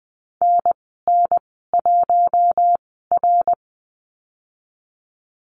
Morse code, heard as a keyed tone, about 20 wpm, spaced as for 14 wpm, 700 Hz. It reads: NN1R
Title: 069 Lesson 12g - Letter d - Call Signs 20wpm